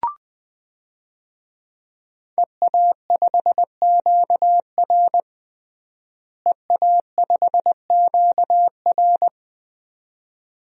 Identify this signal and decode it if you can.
Morse code, heard as a keyed tone, about 20 wpm, 700 Hz. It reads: EA5QR EA5QR
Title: DXCC-100 Call Signs - 5 Characters - Repeated 2x - 20wpm